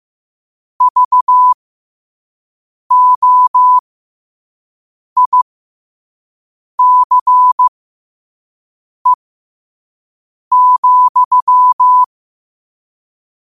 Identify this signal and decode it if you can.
Morse code, heard as a keyed tone, about 15 wpm, spaced as for 5 wpm, 1000 Hz. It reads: VOICE,
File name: qotdfiles / 5wpm